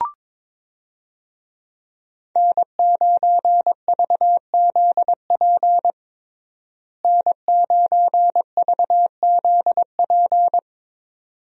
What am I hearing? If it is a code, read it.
Morse code, heard as a keyed tone, about 21 wpm, 700 Hz. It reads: N9VZP N9VZP
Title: USA Call Signs - 5 Characters - Repeated 2x - 22wpm